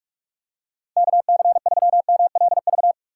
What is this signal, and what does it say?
Morse code, heard as a keyed tone, about 48 wpm, 700 Hz. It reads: KC3MLV
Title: USA Call Signs - 6 Characters - 6x Character Spacing - 45wpm